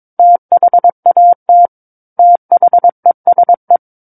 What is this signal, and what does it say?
Morse code, heard as a keyed tone, about 22 wpm, 700 Hz. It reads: THAT THESE
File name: Book One / 13-Chapter 13 - 22 WPM